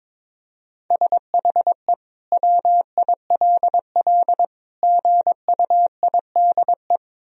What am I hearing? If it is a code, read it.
Morse code, heard as a keyed tone, about 22 wpm, 700 Hz. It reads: SHE WILL GUIDE
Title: Sentences from Top 1000 Words - Review Part 3 - 22wpm